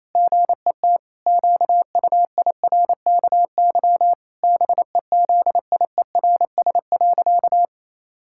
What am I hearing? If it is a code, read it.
Morse code, heard as a keyed tone, about 28 wpm, 700 Hz. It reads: GET QUIRKY BEZIERS.
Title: Pangram Challenge v2 - 28wpm